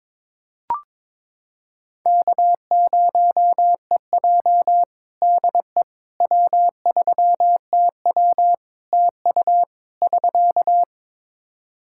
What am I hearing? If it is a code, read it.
Morse code, heard as a keyed tone, about 22 wpm, 700 Hz. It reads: K0EJ DE W3TW TU <SK>
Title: Practice QSO - Exchange 4 - Ending - 22wpm